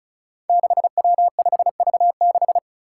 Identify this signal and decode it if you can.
Morse code, heard as a keyed tone, about 35 wpm, 700 Hz. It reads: 6W5V6
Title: Five Challenging Letter-Number Combinations 35wpm